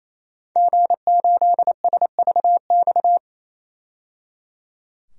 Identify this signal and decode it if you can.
Morse code, heard as a keyed tone, about 28 wpm, 700 Hz. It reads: G8SVX
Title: DXCC-100 Call Signs - 5 Characters - 28wpm